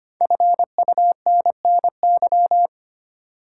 Morse code, heard as a keyed tone, about 25 wpm, 700 Hz. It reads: FUNNY
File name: 5 Letter Words - First 3 Letters Given - 25wpm